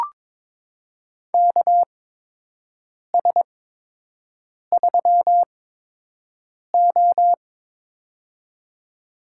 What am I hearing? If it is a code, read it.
Morse code, heard as a keyed tone, about 22 wpm, spaced as for 6 wpm, 700 Hz. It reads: KS3O
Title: USA Call Signs - 4 Characters - 8x Character Spacing - 22wpm